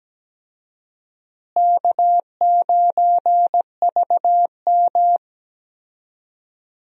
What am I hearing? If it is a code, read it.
Morse code, heard as a keyed tone, about 17 wpm, 700 Hz. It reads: K9VM
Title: USA Call Signs - 4 Characters - Repeated 3x - 17wpm